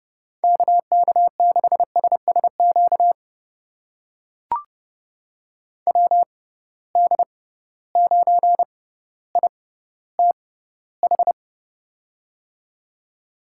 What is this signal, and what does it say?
Morse code, heard as a keyed tone, about 30 wpm, spaced as for 15 wpm, 700 Hz. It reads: KK6SSQ WD9ITH
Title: USA Call Signs - 6 Characters - 6x Character Spacing - 30wpm